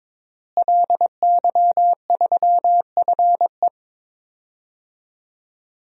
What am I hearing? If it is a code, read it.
Morse code, heard as a keyed tone, about 22 wpm, 700 Hz. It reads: LY3FE
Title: DXCC-100 Call Signs - 5 Characters - Repeated 2x - 22wpm